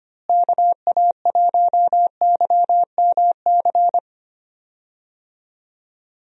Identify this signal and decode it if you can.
Morse code, heard as a keyed tone, about 25 wpm, 700 Hz. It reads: KA1YMC
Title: USA Call Signs - 6 Characters - Repeated 3x - 25wpm